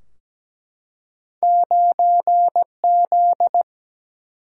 Morse code, heard as a keyed tone, about 17 wpm, 700 Hz. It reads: 9Z
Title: Single-Letter Single-Number 17wpm